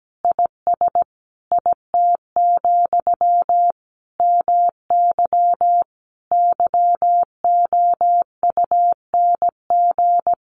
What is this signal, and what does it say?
Morse code, heard as a keyed tone, about 17 wpm, 700 Hz. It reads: IS IT, MY YOUNG
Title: Sentences from Top 1100 Words 17wpm